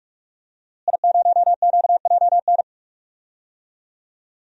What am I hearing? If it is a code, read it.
Morse code, heard as a keyed tone, about 46 wpm, 700 Hz. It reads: I0QJN
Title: DXCC-100 Call Signs - 5 Characters - Repeated 3x - 45wpm